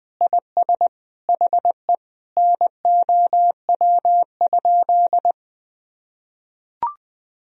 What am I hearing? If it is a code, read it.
Morse code, heard as a keyed tone, about 20 wpm, 700 Hz. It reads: IS HE NOW?
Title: Sentences from Top 200 Words and Abbreviations 20wpm